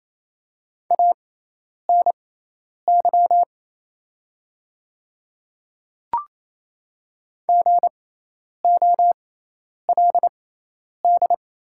Morse code, heard as a keyed tone, about 28 wpm, spaced as for 10 wpm, 700 Hz. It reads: ANY GOLD
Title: 6x Character Spacing - Top 500 Words - 28wpm